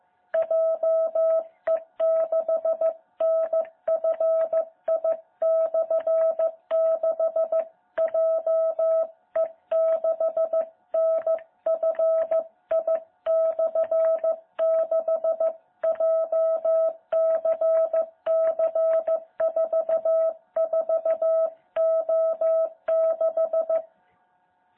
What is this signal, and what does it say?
Morse code, heard as a keyed tone, about 15 wpm, 660 Hz. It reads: JE6NFI/6 JE6NFI/6 JCC44O6
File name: jh8cbh / 2021cw